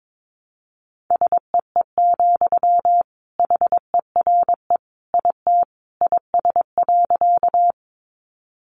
Morse code, heard as a keyed tone, about 22 wpm, 700 Hz. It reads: SEE, HERE IT IS.
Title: Sentences from Top 200 Words 22wpm